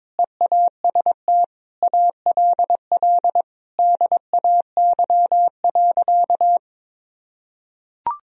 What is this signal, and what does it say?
Morse code, heard as a keyed tone, about 22 wpm, 700 Hz. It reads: EAST ALL DAY.
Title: Sentences from Top 1000 Words - Review Part 3 - 22wpm